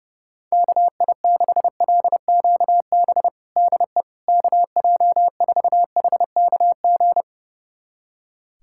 Morse code, heard as a keyed tone, about 30 wpm, 700 Hz. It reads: KI6LQB DE KJ4HKG
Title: USA Callsign DE Callsign - 6 Characters - 30wpm